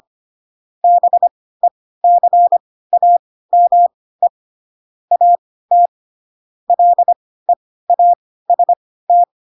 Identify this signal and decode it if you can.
Morse code, heard as a keyed tone, about 25 wpm, spaced as for 16 wpm, 700 Hz. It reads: BECAME AT LEAST
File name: Pinocchio / Pinocchio ch5 10 25 ik1hge